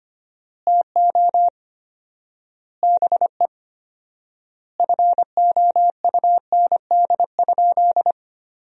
Morse code, heard as a keyed tone, about 25 wpm, 700 Hz. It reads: TO BE FOUND?
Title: 4x Word-Spacing - Sentences from Top 300 Words - 25wpm